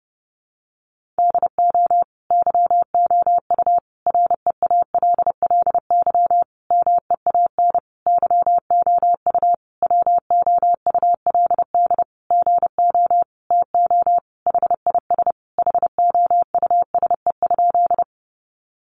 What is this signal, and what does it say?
Morse code, heard as a keyed tone, about 30 wpm, 700 Hz. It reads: DO YOU REALLY MEAN YOU WOULD GO TO HIS HOUSE?
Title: Sentences from Top 600 Words 30wpm